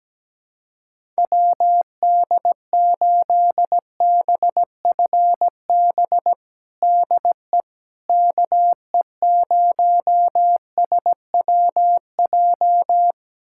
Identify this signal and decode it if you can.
Morse code, heard as a keyed tone, about 17 wpm, 700 Hz. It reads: WD8BFB DE KE0SWJ
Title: USA Callsign DE Callsign - 6 Characters - 17wpm